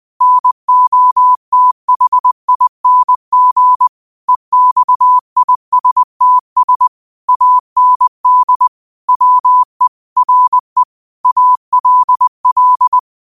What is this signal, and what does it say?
Morse code, heard as a keyed tone, about 20 wpm, 1000 Hz. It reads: NOTHING EXISTS AND WE RE ALL